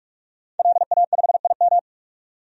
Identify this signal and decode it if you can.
Morse code, heard as a keyed tone, about 43 wpm, 700 Hz. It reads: LA5IM
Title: DXCC-100 Call Signs - 5 Characters - 45wpm